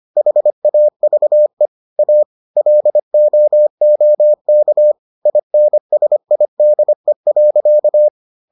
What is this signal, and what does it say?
Morse code, heard as a keyed tone, about 25 wpm, 600 Hz. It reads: HAVE A LOOK INSIDE.